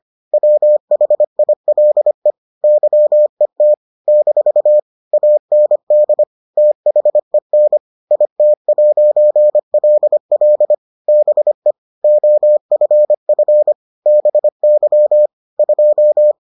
Morse code, heard as a keyed tone, about 25 wpm, 600 Hz. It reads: WHILE YET - AND THEN IT'LL BE OFF BY 2